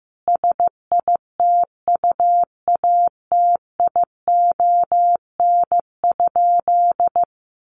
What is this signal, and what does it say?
Morse code, heard as a keyed tone, about 15 wpm, 700 Hz. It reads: SITUATION?
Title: Sentences from Top 1200 Words 15wpm